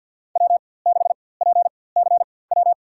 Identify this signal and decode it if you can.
Morse code, heard as a keyed tone, about 54 wpm, spaced as for 24 wpm, 700 Hz. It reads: W6PCW
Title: USA Call Signs - 4x Character Spacing - 50wpm